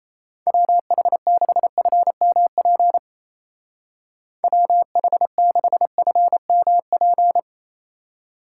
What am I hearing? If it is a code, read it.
Morse code, heard as a keyed tone, about 34 wpm, 700 Hz. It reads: WH6FMP WH6FMP
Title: Speed Racing - USA 2x3 Call Signs 28wpm